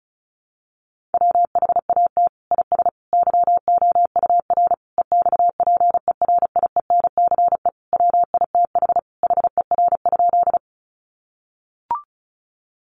Morse code, heard as a keyed tone, about 34 wpm, 700 Hz. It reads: WHAT IS YOUR EXPERIENCE WITH HER?
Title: Sentences from Top 600 Words 35wpm